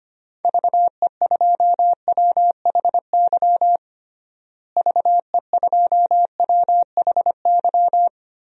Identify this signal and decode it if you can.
Morse code, heard as a keyed tone, about 25 wpm, 700 Hz. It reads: VE2WHY VE2WHY
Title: DXCC-100 Call Signs - 6 Characters - Repeated 3x - 25wpm